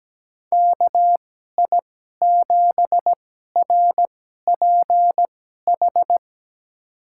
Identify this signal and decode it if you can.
Morse code, heard as a keyed tone, about 17 wpm, spaced as for 12 wpm, 700 Hz. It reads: KI7RPH
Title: USA Call Signs - 6 Characters - 2x Character Spacing - 17wpm